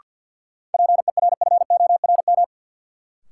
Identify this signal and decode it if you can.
Morse code, heard as a keyed tone, about 52 wpm, 700 Hz. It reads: PERFORM
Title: Sentences from Top 1000 Words - Review Part 5 - New Word Drill - 50wpm